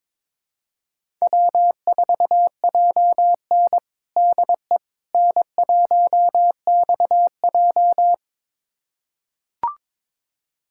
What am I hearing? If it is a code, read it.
Morse code, heard as a keyed tone, about 21 wpm, 700 Hz. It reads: W4JN DE N1XJ
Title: USA Callsign DE Callsign - 4 Characters - 22wpm